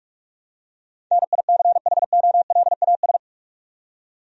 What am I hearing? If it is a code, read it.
Morse code, heard as a keyed tone, about 46 wpm, 700 Hz. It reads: NICHOLAS
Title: USA Common Names - 45wpm